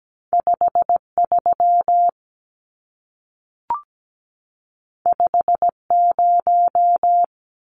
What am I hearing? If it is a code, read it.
Morse code, heard as a keyed tone, about 17 wpm, 700 Hz. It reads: 53 50
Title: Numbers - 2-Digit 17wpm